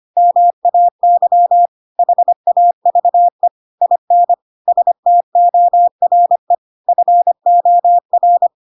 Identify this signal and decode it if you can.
Morse code, heard as a keyed tone, about 25 wpm, 700 Hz. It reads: MAY HAVE IN STORE FOR
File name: Pinocchio / Pinocchio ch7 25 25 ik1hge